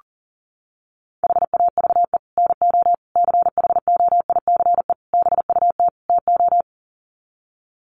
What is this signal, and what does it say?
Morse code, heard as a keyed tone, about 40 wpm, 700 Hz. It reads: HAVE NO CHOICE BUT TO
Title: Sets of 5 Words Encore 40wpm